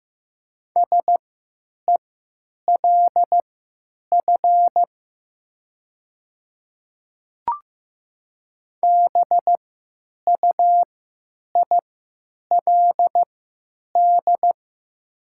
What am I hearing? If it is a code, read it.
Morse code, heard as a keyed tone, about 15 wpm, spaced as for 9 wpm, 700 Hz. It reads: SELF BUILD
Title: 3x Character Spacing - Top 200 Words - 15wpm